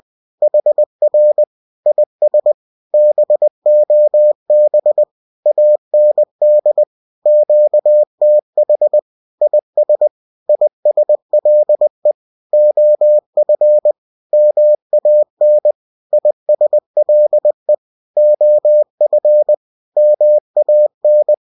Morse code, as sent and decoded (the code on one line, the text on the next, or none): .... .-. / .. ... / -... --- -... / .- -. -.. / --.- - .... / .. ... / .. ... .-.. / --- ..-. / -- .- -. / .. ... .-.. / --- ..-. / -- .- -.
HR IS BOB AND QTH IS ISLE OF MAN ISLE OF MAN